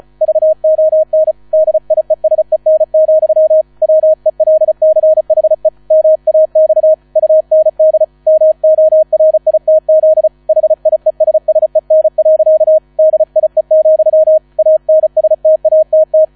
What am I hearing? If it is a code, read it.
Morse code, heard as a keyed tone, about 34 wpm, 630 Hz. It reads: VON DIESEN, WELCHE MAX UND MORITZ HIESSEN. DIE, ANSTATT